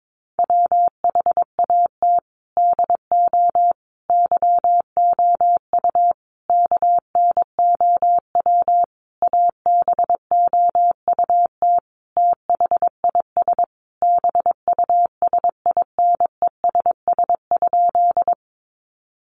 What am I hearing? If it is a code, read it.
Morse code, heard as a keyed tone, about 22 wpm, 700 Hz. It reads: WHAT DO YOU KNOW ABOUT THIS BUSINESS?
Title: Sentences from Top 600 Words 22wpm